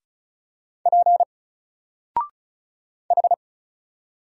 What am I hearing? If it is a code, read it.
Morse code, heard as a keyed tone, about 34 wpm, 700 Hz. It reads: P H
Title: Single Letter-Number 35wpm